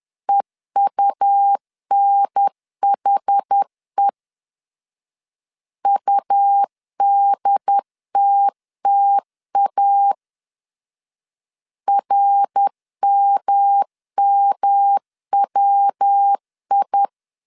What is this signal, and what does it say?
Morse code, heard as a keyed tone, about 10 wpm, 800 Hz. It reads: EUNHE UDTTA RMMWI
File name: CW / DET / 1 3 1500C S B 10wpm 3335